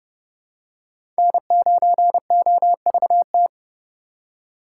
Morse code, heard as a keyed tone, about 30 wpm, 700 Hz. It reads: N9OVT
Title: USA Call Signs - 4x Character Spacing - 30wpm